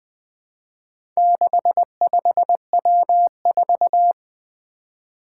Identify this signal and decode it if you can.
Morse code, heard as a keyed tone, about 20 wpm, 700 Hz. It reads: 65W4
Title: Four Challenging Letter-Number Combinations 20wpm